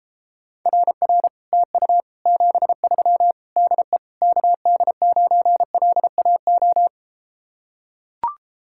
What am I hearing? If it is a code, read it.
Morse code, heard as a keyed tone, about 33 wpm, 700 Hz. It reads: RR TU 73 DE KD9LAO